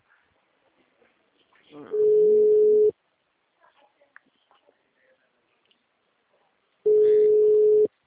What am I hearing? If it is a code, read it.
Ringback tone (Europe, 425 Hz), with speech.